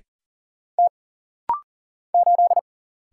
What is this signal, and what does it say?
Morse code, heard as a keyed tone, about 39 wpm, 700 Hz. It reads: T 8
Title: Single Letter-Number - Rapid-Fire 40wpm